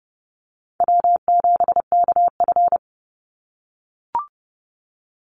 Morse code, heard as a keyed tone, about 30 wpm, 700 Hz. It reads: W7KF
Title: Easy Call Signs - 30wpm